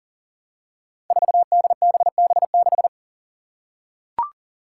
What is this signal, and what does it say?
Morse code, heard as a keyed tone, about 40 wpm, 700 Hz. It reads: 4DBB6
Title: Five Challenging Letter-Number Combinations 40wpm